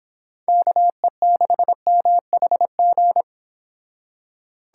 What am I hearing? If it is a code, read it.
Morse code, heard as a keyed tone, about 25 wpm, 700 Hz. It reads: KE6MHG